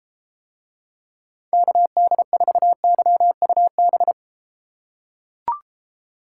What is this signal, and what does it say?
Morse code, heard as a keyed tone, about 29 wpm, 700 Hz. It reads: KD4YUB